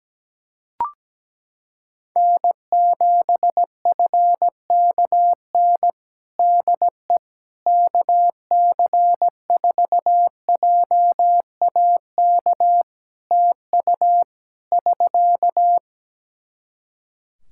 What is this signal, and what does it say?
Morse code, heard as a keyed tone, about 17 wpm, 700 Hz. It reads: N7FKN DE KC4JAK TU <SK>
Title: Practice QSO - Exchange 4 - Ending - 17wpm